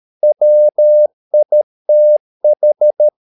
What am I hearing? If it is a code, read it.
Morse code, heard as a keyed tone, about 13 wpm, 600 Hz. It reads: WITH
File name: mp3 files / Lexiphiles - 13wpm